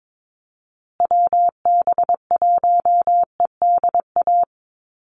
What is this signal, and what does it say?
Morse code, heard as a keyed tone, about 22 wpm, 700 Hz. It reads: WB1EDA
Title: USA Call Signs Repeated 2x 22wpm